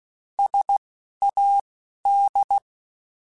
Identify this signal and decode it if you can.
Morse code, heard as a keyed tone, about 16 wpm, spaced as for 11 wpm, 800 Hz. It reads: SAD
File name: Alphabetical / Alphabetical mp3s / Ss @10